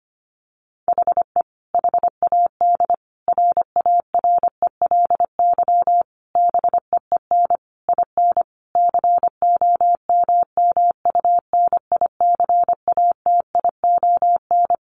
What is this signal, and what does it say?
Morse code, heard as a keyed tone, about 25 wpm, 700 Hz. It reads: HE HAD RARELY BEEN IN COMMUNICATION
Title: Sentences from Top 1400 Words 25wpm